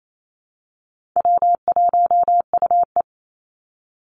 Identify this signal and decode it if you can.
Morse code, heard as a keyed tone, about 28 wpm, 700 Hz. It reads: W1UE
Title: Call Signs - 28wpm